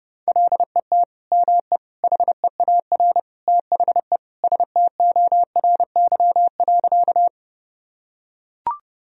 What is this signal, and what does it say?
Morse code, heard as a keyed tone, about 30 wpm, 700 Hz. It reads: LET ME HEAR THE STORY.
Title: Sentences from Top 1000 Words - Review Part 1 - 30wpm